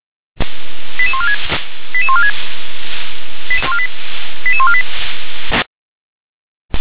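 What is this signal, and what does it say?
Five-tone selective call (ZVEI), heard as quick runs of five tones, tones of about 70 ms. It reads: 88146 88146 88148
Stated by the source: ZVEI five-tone selective call received on an SDR.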